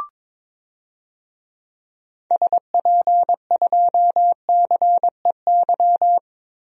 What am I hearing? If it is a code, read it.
Morse code, heard as a keyed tone, about 21 wpm, 700 Hz. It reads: SP2CEY
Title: DXCC-100 Call Signs - 6 Characters - Repeated 3x - 22wpm